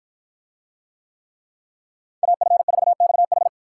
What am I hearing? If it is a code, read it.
Morse code, heard as a keyed tone, about 53 wpm, 700 Hz. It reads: AF4XH